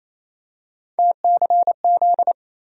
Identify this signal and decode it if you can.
Morse code, heard as a keyed tone, about 28 wpm, 700 Hz. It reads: TCZ
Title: USA Call Sign Postfixes - 3 Letters 28wpm